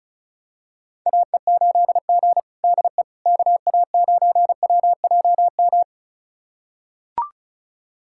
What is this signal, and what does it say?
Morse code, heard as a keyed tone, about 36 wpm, 700 Hz. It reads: AE8G DE KA9WJM
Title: USA Callsign DE Callsign 35wpm